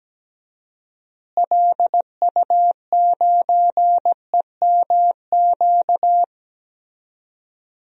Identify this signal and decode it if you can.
Morse code, heard as a keyed tone, about 17 wpm, 700 Hz. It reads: LU9EMQ
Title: DXCC-100 Call Signs - 6 Characters - Repeated 2x - 17wpm